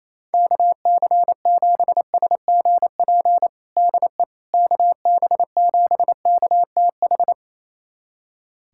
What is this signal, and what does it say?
Morse code, heard as a keyed tone, about 28 wpm, 700 Hz. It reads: KC7SGP DE KB7KTH
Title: USA Callsign DE Callsign 28wpm